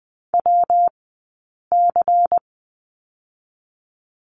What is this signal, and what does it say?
Morse code, heard as a keyed tone, about 20 wpm, 700 Hz. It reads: W C
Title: 137 Lesson 23d - Letter f - Two Characters 20wpm